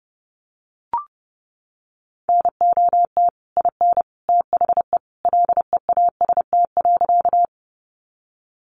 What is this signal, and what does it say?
Morse code, heard as a keyed tone, about 30 wpm, 700 Hz. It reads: NOT IN THE LEAST.
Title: Sentences from Top 700 Words 30wpm